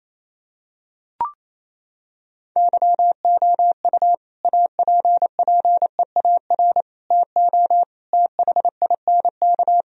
Morse code, heard as a keyed tone, about 28 wpm, 700 Hz. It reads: YOU APPEAR TO THINK
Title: Sentences from Top 1000 Words - Review Part 2 - 28wpm